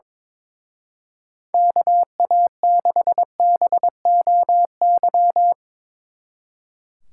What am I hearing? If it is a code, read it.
Morse code, heard as a keyed tone, about 25 wpm, 700 Hz. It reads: KA6BOY
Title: Speed Racing - USA Call Signs 22wpm